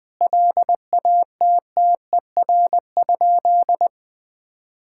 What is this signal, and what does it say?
Morse code, heard as a keyed tone, about 20 wpm, 700 Hz. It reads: LATTER?
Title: Sentences from Top 1600 Words 20wpm